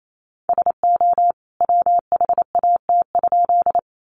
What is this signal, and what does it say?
Morse code, heard as a keyed tone, about 28 wpm, 700 Hz. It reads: SO WHAT?
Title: A Sound Of Thunder 28wpm